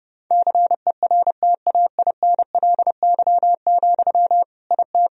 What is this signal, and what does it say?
Morse code, heard as a keyed tone, about 32 wpm, 700 Hz. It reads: CERTAINLY, IT
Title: Sentences from Top 1000 Words - Review Part 2 - 30wpm